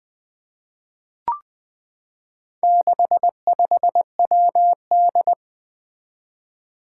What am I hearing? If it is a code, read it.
Morse code, heard as a keyed tone, about 20 wpm, 700 Hz. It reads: 65WD
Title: Four Challenging Letter-Number Combinations 20wpm